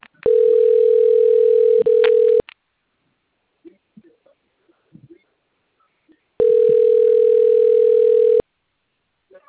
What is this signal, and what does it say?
Ringback tone (North America), 440 + 480 Hz.